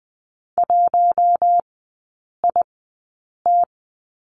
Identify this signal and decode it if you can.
Morse code, heard as a keyed tone, about 20 wpm, spaced as for 8 wpm, 700 Hz. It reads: 1IT
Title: 047 Lesson 9e - Number 4 - Three Characters 20wpm